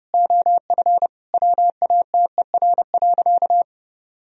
Morse code, heard as a keyed tone, about 30 wpm, 700 Hz. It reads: OF WATER.
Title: Sentences from Top 1000 Words - Review Part 1 - 30wpm